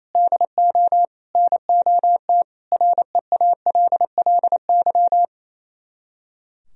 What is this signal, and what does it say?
Morse code, heard as a keyed tone, about 30 wpm, 700 Hz. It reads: DO NOT REALLY
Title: Speed-Racing - Sets of 4 Words 28wpm